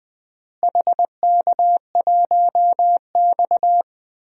Morse code, heard as a keyed tone, about 20 wpm, 700 Hz. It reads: HK1X
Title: DXCC-100 Call Signs - 4 Characters - Repeated 3x - 20wpm